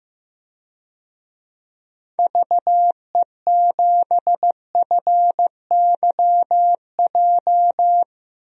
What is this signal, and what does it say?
Morse code, heard as a keyed tone, about 15 wpm, 700 Hz. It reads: VE7FYJ
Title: DXCC-100 Call Signs - 6 Characters - 15wpm